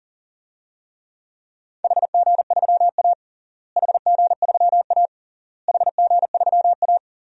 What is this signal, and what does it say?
Morse code, heard as a keyed tone, about 40 wpm, 700 Hz. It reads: HG3A HG3A HG3A
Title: DXCC-100 Call Signs - 4 Characters - Repeated 3x - 40wpm